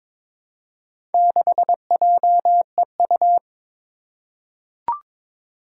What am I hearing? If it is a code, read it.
Morse code, heard as a keyed tone, about 22 wpm, 700 Hz. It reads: 6JEU
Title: Four Challenging Letter-Number Combinations 22wpm